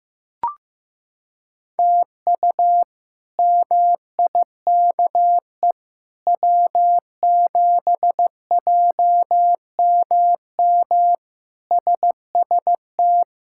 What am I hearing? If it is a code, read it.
Morse code, heard as a keyed tone, about 15 wpm, 700 Hz. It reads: TU MIKE W7JMM SST
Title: K1USN SST QSO Practice - 15wpm